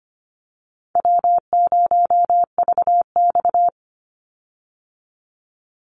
Morse code, heard as a keyed tone, about 25 wpm, 700 Hz. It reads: W0VX
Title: Call Signs Repeated 2x - 25wpm